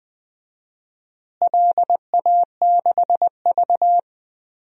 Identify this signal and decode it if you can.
Morse code, heard as a keyed tone, about 20 wpm, 700 Hz. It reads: LA6V
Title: DXCC-100 Call Signs - 4 Characters - Repeated 2x - 20wpm